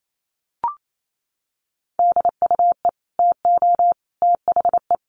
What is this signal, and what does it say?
Morse code, heard as a keyed tone, about 28 wpm, 700 Hz. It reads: DUE TO THE
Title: Sets of 5 Words Encore 28wpm